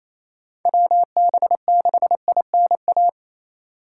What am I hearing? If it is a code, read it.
Morse code, heard as a keyed tone, about 36 wpm, 700 Hz. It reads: WB6INA